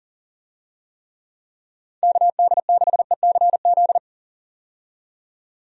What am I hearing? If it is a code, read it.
Morse code, heard as a keyed tone, about 40 wpm, 700 Hz. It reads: KD6ECZ
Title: USA Call Signs - 6 Characters - 40wpm